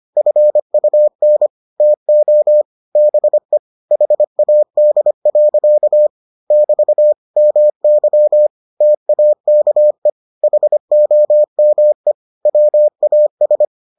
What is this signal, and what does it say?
Morse code, heard as a keyed tone, about 25 wpm, 600 Hz. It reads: FUN TO BE HAD. <BT> MY TAKE HOME WAS